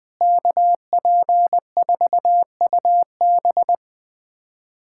Morse code, heard as a keyed tone, about 21 wpm, 700 Hz. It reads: KP4UB
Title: USA Callsign DE Callsign 20wpm